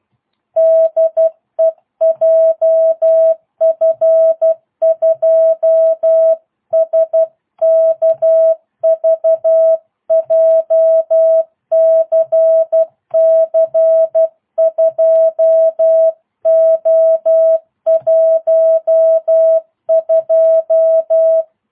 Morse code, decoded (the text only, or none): DE JF2SKV JCC 2O12